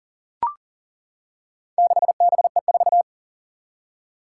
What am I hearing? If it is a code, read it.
Morse code, heard as a keyed tone, about 40 wpm, 700 Hz. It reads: 6BE4
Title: Four Challenging Letter-Number Combinations 40wpm